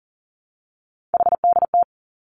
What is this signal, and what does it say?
Morse code, heard as a keyed tone, about 42 wpm, 700 Hz. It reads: HDT
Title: Three Challenging Letter-Number Combinations 40wpm